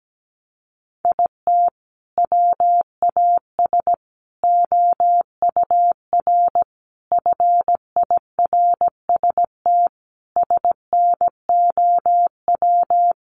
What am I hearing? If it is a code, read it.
Morse code, heard as a keyed tone, about 17 wpm, 700 Hz. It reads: IT WAS OUR FIRST SNOW
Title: Sentences from Top 1500 Words 17wpm